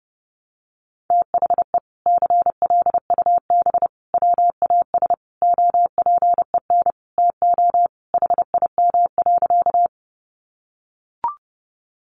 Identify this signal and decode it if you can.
Morse code, heard as a keyed tone, about 30 wpm, 700 Hz. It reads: THE CLUB WAS OPEN TO HIM.
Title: Sentences from Top 1200 Words 30wpm